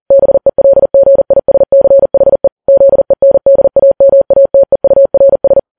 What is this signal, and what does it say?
Morse code, heard as a keyed tone, about 40 wpm, 550 Hz. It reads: BELGISCHE ZENDAMATEURS
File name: Files Web tls / Morse on9fv / morse40kps